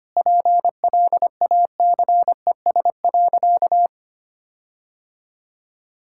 Morse code, heard as a keyed tone, about 25 wpm, 700 Hz. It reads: PLACES.
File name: practice / 25wpm / Sentences with 4-Word Phrases - 25wpm